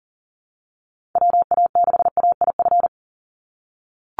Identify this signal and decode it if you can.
Morse code, heard as a keyed tone, about 40 wpm, 690 Hz. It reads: WA6AIF
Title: USA Call Signs 40wpm